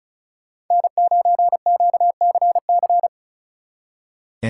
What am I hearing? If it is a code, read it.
Morse code, heard as a keyed tone, about 47 wpm, 700 Hz. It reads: N9QCC